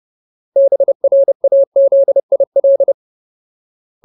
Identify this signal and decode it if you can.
Morse code, heard as a keyed tone, about 30 wpm, 550 Hz. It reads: BRAZIL
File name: Ultimate Word List / 30WPM Ultimate